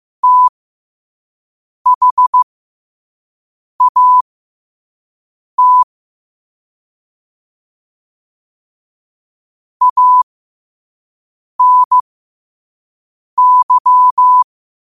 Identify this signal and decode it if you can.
Morse code, heard as a keyed tone, about 15 wpm, spaced as for 5 wpm, 1000 Hz. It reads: THAT ANY
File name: qotdfiles / 5wpm